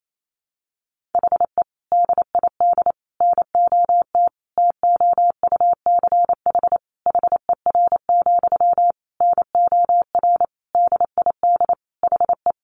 Morse code, heard as a keyed tone, about 28 wpm, 700 Hz. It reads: HE DID NOT TOUCH HER, NOR DID HE
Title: Sentences from Top 1500 Words 28wpm